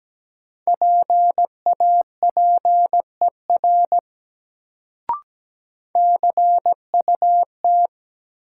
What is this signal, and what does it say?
Morse code, heard as a keyed tone, about 17 wpm, 700 Hz. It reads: PAPER CUT
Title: Sentences from Top 300 Words - New Word Drill - 17wpm